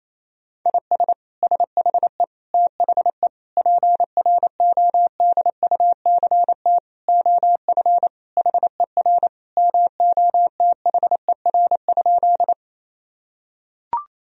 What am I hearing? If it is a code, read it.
Morse code, heard as a keyed tone, about 28 wpm, 700 Hz. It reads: IS SHE THE PRODUCT OF HER MOTHER?
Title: Sentences from Top 1000 Words - Review Part 2 - 28wpm